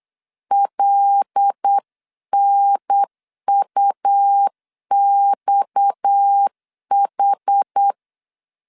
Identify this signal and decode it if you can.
Morse code, heard as a keyed tone, about 8 wpm, 800 Hz. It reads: LNUXH
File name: LEZIONI / 6 PKX0 / 6 PKX 8wpm 3041